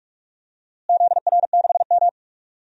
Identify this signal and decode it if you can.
Morse code, heard as a keyed tone, about 48 wpm, 700 Hz. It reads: ZR6M
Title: DXCC-100 Call Signs - 4 Characters - Repeated 2x - 45wpm